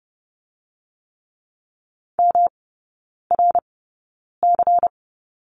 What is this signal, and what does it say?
Morse code, heard as a keyed tone, about 30 wpm, spaced as for 9 wpm, 700 Hz. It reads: MRC
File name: Lesson 18e - Letter m - Three Characters 30wpm